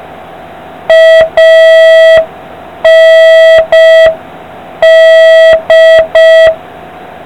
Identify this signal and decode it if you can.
Morse code, heard as a keyed tone, about 5 wpm, 1950 Hz. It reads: AND